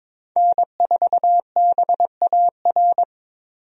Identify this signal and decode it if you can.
Morse code, heard as a keyed tone, about 22 wpm, 700 Hz. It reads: N4BAR
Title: USA Call Signs - 5 Characters - 22wpm